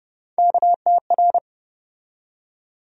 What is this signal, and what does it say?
Morse code, heard as a keyed tone, about 30 wpm, 700 Hz. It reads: KTR
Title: USA Call Sign Postfixes - 3 Letters 30wpm